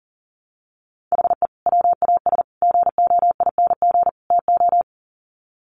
Morse code, heard as a keyed tone, about 40 wpm, 700 Hz. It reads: HE WAS GOING TO
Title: Sets of 4 Words 40wpm